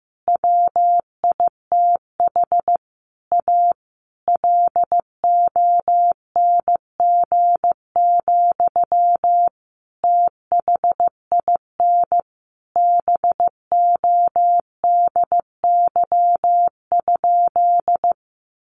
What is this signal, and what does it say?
Morse code, heard as a keyed tone, about 15 wpm, 700 Hz. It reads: WITH A LONG, THIN BODY?
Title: Sentences from Top 1200 Words 15wpm